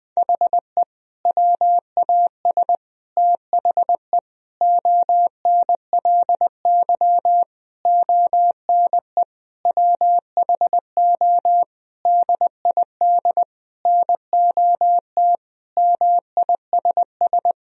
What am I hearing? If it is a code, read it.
Morse code, heard as a keyed tone, about 19 wpm, 700 Hz. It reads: HE WAS THE ONLY ONE WHO DID NOT MISS
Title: Sentences from Top 1000 Words - Review Part 3 - 20wpm